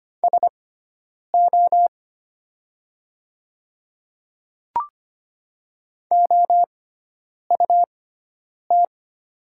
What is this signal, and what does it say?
Morse code, heard as a keyed tone, about 25 wpm, spaced as for 9 wpm, 700 Hz. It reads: SO OUT
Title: 6x Character Spacing - Top 100 Words - 25wpm